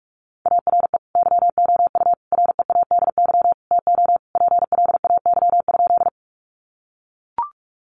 Morse code, heard as a keyed tone, about 44 wpm, 700 Hz. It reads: ARE YOU READY TO PLAY?
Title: Sentences from Top 400 Words 45wpm